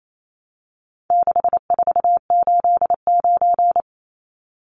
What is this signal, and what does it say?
Morse code, heard as a keyed tone, about 28 wpm, 700 Hz. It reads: 6489
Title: Numbers - 4-Digit 28wpm